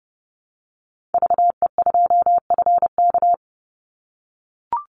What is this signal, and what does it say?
Morse code, heard as a keyed tone, about 30 wpm, 700 Hz. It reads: VE2FK
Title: Call Signs - 30wpm